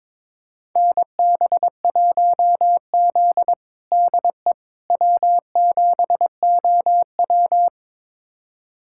Morse code, heard as a keyed tone, about 22 wpm, 700 Hz. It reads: NB1Z DE W7OW
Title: USA Callsign DE Callsign - 4 Characters - 22wpm